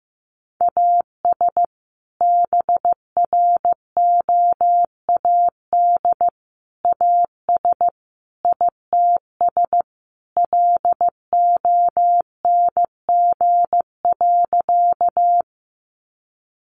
Morse code, heard as a keyed tone, about 15 wpm, 700 Hz. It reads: AS BROAD AS ITS LONG.
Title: Sentences from Top 1000 Words 15wpm